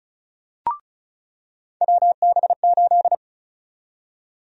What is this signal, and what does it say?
Morse code, heard as a keyed tone, about 36 wpm, 700 Hz. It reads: WB8
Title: USA Call Sign Prefixes - 35wpm